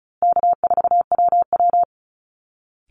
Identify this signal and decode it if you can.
Morse code, heard as a keyed tone, about 34 wpm, 700 Hz. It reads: K4WW
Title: Easy Call Signs Repeated 3x - 35wpm